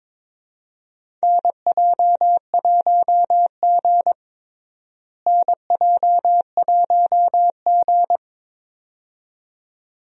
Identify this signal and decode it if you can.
Morse code, heard as a keyed tone, about 22 wpm, 700 Hz. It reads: NJ1G NJ1G
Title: USA Call Signs - 4 Characters - Repeated 2x - 22wpm